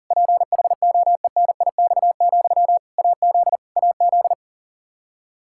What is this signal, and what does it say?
Morse code, heard as a keyed tone, about 37 wpm, 700 Hz. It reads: PHOENIX, AZ AZ
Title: K1USN SST QSO Practice 40wpm